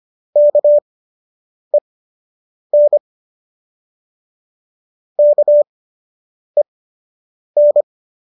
Morse code, heard as a keyed tone, about 25 wpm, spaced as for 8 wpm, 590 Hz. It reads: KEN KEN
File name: qso003 08